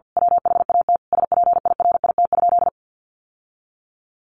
Morse code, heard as a keyed tone, about 50 wpm, 700 Hz. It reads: WHAT SPIRIT?
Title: Sentences from Top 1400 Words 50wpm